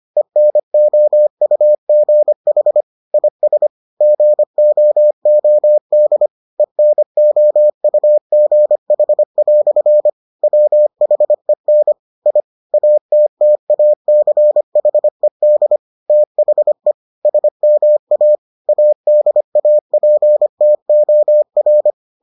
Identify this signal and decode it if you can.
Morse code, heard as a keyed tone, about 25 wpm, 600 Hz. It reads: ENOUGH IS GOOD ENOUGH" WHEN I ATTACHED THE SMA ADAPTOR